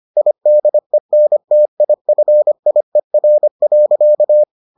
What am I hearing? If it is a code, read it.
Morse code, heard as a keyed tone, about 25 wpm, 600 Hz. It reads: IDENTIFIER.